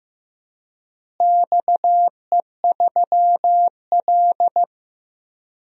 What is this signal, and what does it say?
Morse code, heard as a keyed tone, about 15 wpm, 700 Hz. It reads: XE3L
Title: DXCC-100 Call Signs - 4 Characters - Repeated 3x - 15wpm